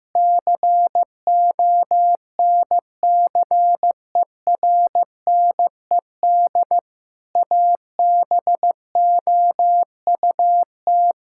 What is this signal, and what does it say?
Morse code, heard as a keyed tone, about 15 wpm, 700 Hz. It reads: CONCERNED ABOUT
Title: Sentences from Top 1000 Words - Review Part 6 - 15wpm